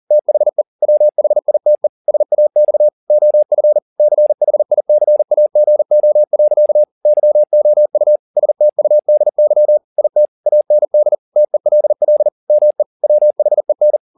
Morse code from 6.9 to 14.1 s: YOU STUDY IT AND TELL ME WHEN

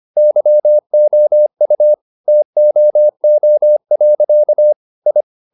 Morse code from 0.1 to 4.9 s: YOU TOO.

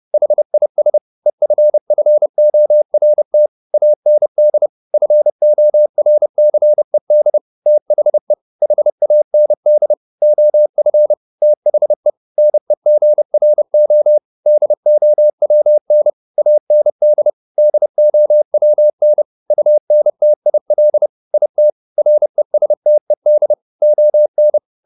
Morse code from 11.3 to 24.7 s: THE NEGRO DOWN AND DOWN UNTIL IT RESTED ON

